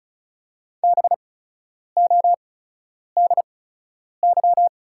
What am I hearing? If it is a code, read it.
Morse code, heard as a keyed tone, about 34 wpm, spaced as for 10 wpm, 690 Hz. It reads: BODY
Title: 8x Character Spacing - Top 500 Words - 35wpm